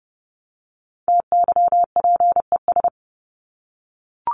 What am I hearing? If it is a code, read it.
Morse code, heard as a keyed tone, about 30 wpm, 700 Hz. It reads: TYPES
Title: ES:S Suffix - Top 100 Words with Base Words 30wpm